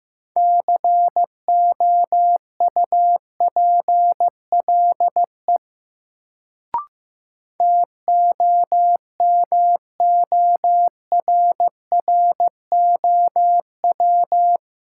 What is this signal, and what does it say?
Morse code, heard as a keyed tone, about 15 wpm, 700 Hz. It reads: COUPLE TOMORROW
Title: Sentences from Top 1100 Words - New Word Drill Alt - 15wpm